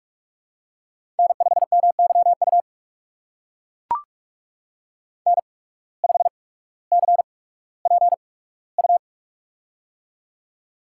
Morse code, heard as a keyed tone, about 44 wpm, spaced as for 19 wpm, 700 Hz. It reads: N5MYU N5CPU
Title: USA Call Signs - 5 Characters - 8x Character Spacing - 45wpm